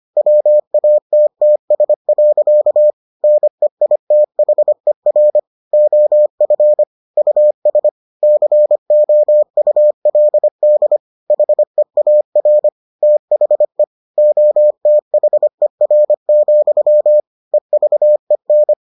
Morse code, heard as a keyed tone, about 25 wpm, 600 Hz. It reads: WATTS. NEITHER OF US COULD HEAR THE OTHER, EVEN